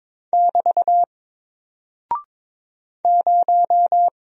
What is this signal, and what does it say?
Morse code, heard as a keyed tone, about 22 wpm, 700 Hz. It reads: <BT> 0
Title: Single Character (Letter:Number:Punctuation:Prosign) Alt 22wpm